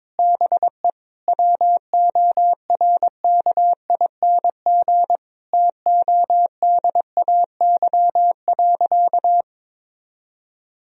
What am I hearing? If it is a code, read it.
Morse code, heard as a keyed tone, about 21 wpm, 700 Hz. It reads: BE WORKING TODAY.
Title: Sentences from Top 1000 Words - Review Part 4 - 22wpm